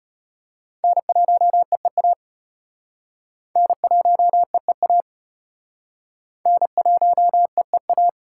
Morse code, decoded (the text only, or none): N1EEA N1EEA N1EEA